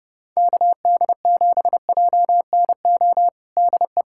Morse code, heard as a keyed tone, about 32 wpm, 700 Hz. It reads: KD7JNO DE
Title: USA Callsign DE Callsign 30wpm